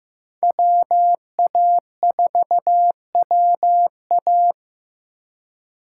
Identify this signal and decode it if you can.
Morse code, heard as a keyed tone, about 15 wpm, 700 Hz. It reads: WA4WA
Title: USA Call Signs - 5 Characters - 3x Character Spacing - 15wpm